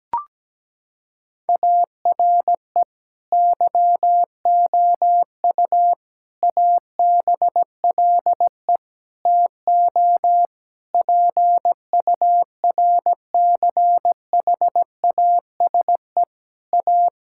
Morse code, heard as a keyed tone, about 17 wpm, 700 Hz. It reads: ARE YOU ABLE TO PURCHASE A
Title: Sentences from Top 1000 Words - Review Part 5 - 17wpm